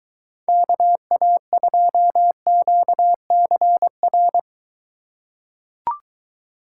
Morse code, heard as a keyed tone, about 22 wpm, 700 Hz. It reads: KA2QCR